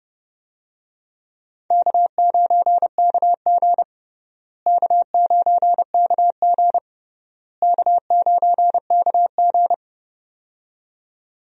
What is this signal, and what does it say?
Morse code, heard as a keyed tone, about 30 wpm, 700 Hz. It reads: K9KG K9KG K9KG
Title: USA Call Signs - 4 Characters - Repeated 3x - 30wpm